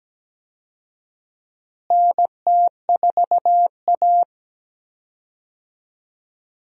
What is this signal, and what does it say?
Morse code, heard as a keyed tone, about 17 wpm, 700 Hz. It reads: NT4A
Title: USA Call Signs - 4 Characters - Repeated 2x - 17wpm